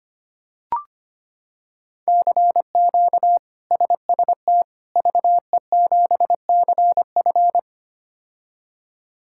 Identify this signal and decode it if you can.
Morse code, heard as a keyed tone, about 25 wpm, 700 Hz. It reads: CQ SST VE7CF
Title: K1USN SST QSO Practice - 25wpm